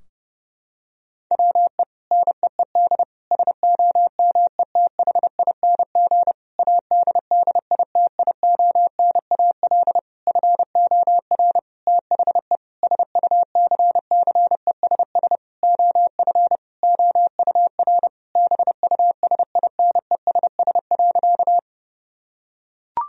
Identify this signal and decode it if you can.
Morse code, heard as a keyed tone, about 30 wpm, 700 Hz. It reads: WE NEED SOMETHING ADDITIONAL FOR THE SUCCESS OF OUR BUSINESS.
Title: Sentences from Top 1000 Words - Review Part 4 - 30wpm